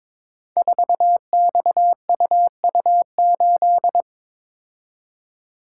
Morse code, heard as a keyed tone, about 22 wpm, 700 Hz. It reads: 4XUU8
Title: Code Groups - 5 Letter:Numbers - 22wpm